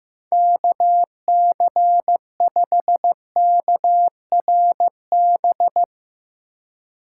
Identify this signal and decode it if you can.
Morse code, heard as a keyed tone, about 15 wpm, 700 Hz. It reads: KC5KRB
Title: USA Call Signs - 6 Characters - 4x Character Spacing - 15wpm